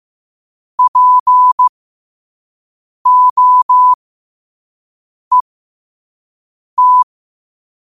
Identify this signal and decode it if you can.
Morse code, heard as a keyed tone, about 15 wpm, spaced as for 5 wpm, 1000 Hz. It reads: POET
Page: New quotes every day in morse code at 5 Words per minute.